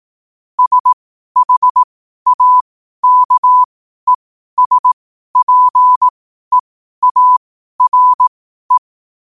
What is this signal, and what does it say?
Morse code, heard as a keyed tone, about 18 wpm, spaced as for 12 wpm, 1000 Hz. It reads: SHAKESPEARE
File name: qotdfiles / 12wpm